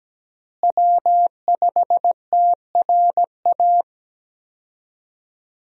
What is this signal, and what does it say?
Morse code, heard as a keyed tone, about 17 wpm, 700 Hz. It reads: W5TRA
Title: USA Call Signs - 5 Characters - Repeated 2x - 17wpm